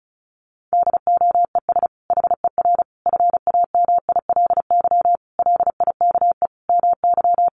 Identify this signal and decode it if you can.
Morse code, heard as a keyed tone, about 34 wpm, 700 Hz. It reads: DOES HER FAMILY LIKE MY
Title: Sentences from Top 1400 Words 35wpm